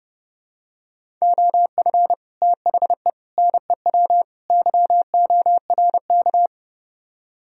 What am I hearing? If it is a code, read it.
Morse code, heard as a keyed tone, about 32 wpm, 700 Hz. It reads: OF THE NEW YORK